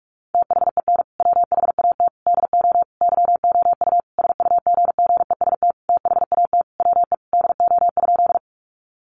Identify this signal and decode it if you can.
Morse code, heard as a keyed tone, about 44 wpm, 700 Hz. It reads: THEN WHAT DO YOU SUGGEST THAT WE DO?
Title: Sentences from Top 1300 Words 45wpm